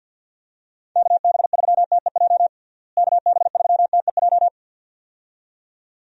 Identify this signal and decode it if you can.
Morse code, heard as a keyed tone, about 50 wpm, 700 Hz. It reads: KB3TEJ KB3TEJ
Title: USA Call Signs - 6 Characters - Repeated 2x - 50wpm